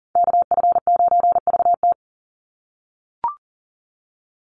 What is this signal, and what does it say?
Morse code, heard as a keyed tone, about 40 wpm, 700 Hz. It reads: KF9VT
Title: USA Call Signs 40wpm